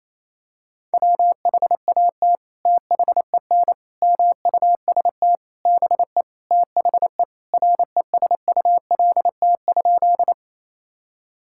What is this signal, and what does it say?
Morse code, heard as a keyed tone, about 28 wpm, 700 Hz. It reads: WHAT THEN MUST BE THE RESULT?
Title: Sentences from Top 1000 Words - Review Part 5 - 28wpm